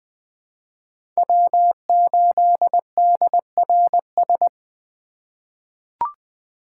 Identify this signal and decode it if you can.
Morse code, heard as a keyed tone, about 21 wpm, 700 Hz. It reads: W8DRS
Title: USA Call Signs - 5 Characters - 3x Character Spacing - 20wpm